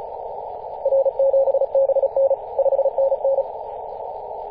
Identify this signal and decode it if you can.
Morse code, heard as a keyed tone, about 34 wpm, 540 Hz, with noise. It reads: R7BN 5NN
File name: 2016/Apr/09 0838z HV0A 18.071MHz CW